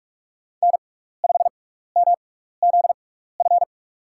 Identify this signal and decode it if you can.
Morse code, heard as a keyed tone, about 44 wpm, spaced as for 15 wpm, 690 Hz. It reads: N5MZF
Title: USA Call Signs - 5 Characters - 6x Character Spacing - 45wpm